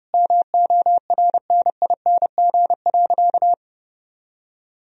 Morse code, heard as a keyed tone, about 30 wpm, 700 Hz. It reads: MORNING.
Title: Sentences from Top 1000 Words - Review Part 2 - 30wpm